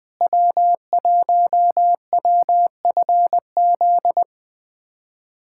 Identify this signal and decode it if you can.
Morse code, heard as a keyed tone, about 20 wpm, 700 Hz. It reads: W1WFZ
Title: USA Callsign DE Callsign - 5 Characters - 20wpm